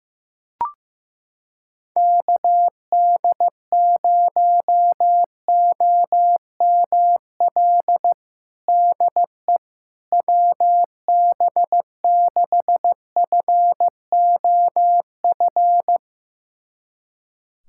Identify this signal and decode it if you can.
Morse code, heard as a keyed tone, about 15 wpm, 700 Hz. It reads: KD0OML DE WB6FOF
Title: USA Callsign DE Callsign 15wpm